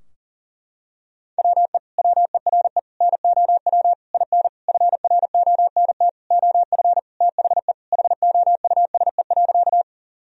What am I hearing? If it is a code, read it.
Morse code, heard as a keyed tone, about 40 wpm, 700 Hz. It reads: WE WERE NOW IN FRONT OF THE HOUSE.
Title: Sentences from Top 1000 Words - Review Part 2 - 40wpm